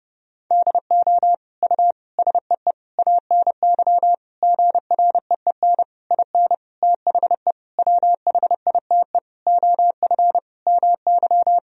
Morse code, heard as a keyed tone, about 32 wpm, 700 Hz. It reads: DO U SEE ANY GREEN IN THE WHITE OF MY
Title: Sentences from Top 500 Words and Abbreviations 30wpm